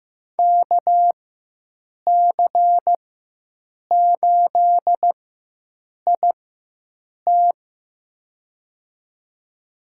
Morse code, heard as a keyed tone, about 15 wpm, spaced as for 7 wpm, 700 Hz. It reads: KC8IT
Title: USA Call Signs - 4x Character Spacing - 15wpm